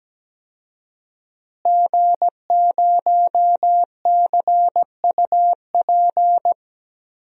Morse code, heard as a keyed tone, about 17 wpm, 700 Hz. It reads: G0CUP
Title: DXCC-100 Call Signs - 5 Characters - Repeated 3x - 17wpm